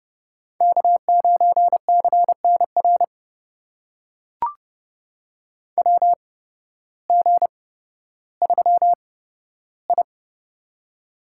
Morse code, heard as a keyed tone, about 30 wpm, spaced as for 14 wpm, 700 Hz. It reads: K9CNR WG3I
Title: USA Call Signs - 8x Character Spacing - 30wpm